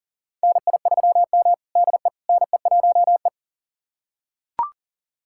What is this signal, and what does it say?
Morse code, heard as a keyed tone, about 40 wpm, 700 Hz. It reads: NI3M DE NE1E
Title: USA Callsign DE Callsign - 4 Characters - 40wpm